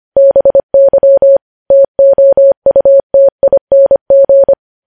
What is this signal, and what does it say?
Morse code, heard as a keyed tone, about 25 wpm, 550 Hz. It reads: BY TOUTING
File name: podcast / News headlines-1x-25x25